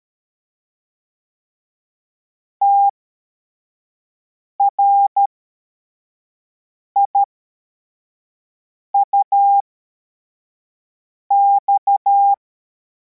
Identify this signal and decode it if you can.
Morse code, heard as a keyed tone, about 13 wpm, spaced as for 4 wpm, 800 Hz. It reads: TRIUX
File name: formation / cw / Text55